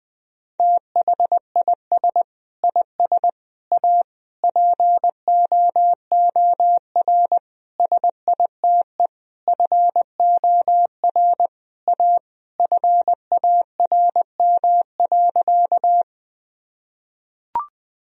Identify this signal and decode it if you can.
Morse code, heard as a keyed tone, about 20 wpm, 700 Hz. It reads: THIS IS A POOR SITE FOR A FARM.
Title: Sentences from Top 1000 Words - Review Part 4 - 20wpm